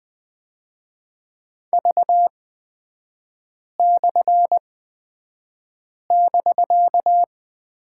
Morse code, heard as a keyed tone, about 20 wpm, spaced as for 5 wpm, 700 Hz. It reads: V/<BK>
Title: 250 Lesson 40e - BK - Three Characters 20wpm